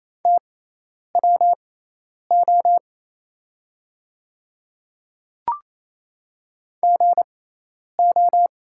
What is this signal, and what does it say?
Morse code, heard as a keyed tone, about 28 wpm, spaced as for 10 wpm, 700 Hz. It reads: TWO GO
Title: 6x Character Spacing - Top 100 Words - 28wpm